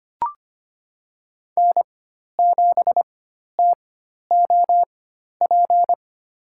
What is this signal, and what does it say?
Morse code, heard as a keyed tone, about 25 wpm, spaced as for 12 wpm, 700 Hz. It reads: N7TOP
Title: USA Call Signs - 4x Character Spacing - 25wpm